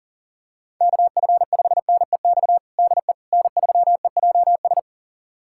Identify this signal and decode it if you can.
Morse code, heard as a keyed tone, about 40 wpm, 700 Hz. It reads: KF5NEX DE N3EJS
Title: USA Callsign DE Callsign 40wpm